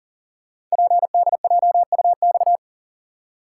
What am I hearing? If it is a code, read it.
Morse code, heard as a keyed tone, about 39 wpm, 700 Hz. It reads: PDJUX
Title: Code Groups - 5 Letters - 40wpm